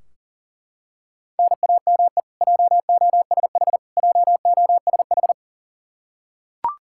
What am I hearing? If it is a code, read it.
Morse code, heard as a keyed tone, about 40 wpm, 700 Hz. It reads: NAME JOSH JOSH
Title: K1USN SST QSO Practice 40wpm